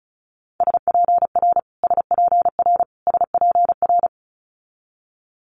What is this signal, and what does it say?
Morse code, heard as a keyed tone, about 36 wpm, 700 Hz. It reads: SPR SPR SPR
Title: Multi-letter Phonemes Repeated 3x 35wpm